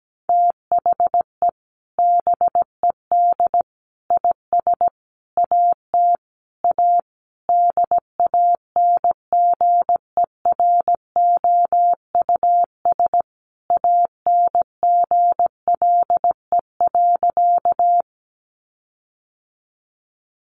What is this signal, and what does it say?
Morse code, heard as a keyed tone, about 17 wpm, 700 Hz. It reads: THE BED IS AT A DANGEROUS ANGLE.
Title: Sentences from Top 1500 Words 17wpm